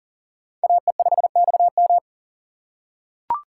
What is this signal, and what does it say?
Morse code, heard as a keyed tone, about 39 wpm, 700 Hz. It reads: AE5XM
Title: USA Call Signs - 5 Characters - 4x Character Spacing - 40wpm